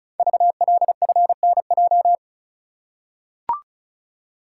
Morse code, heard as a keyed tone, about 36 wpm, 690 Hz. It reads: VLFNJ
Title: Code Groups - 5 Letters - 35wpm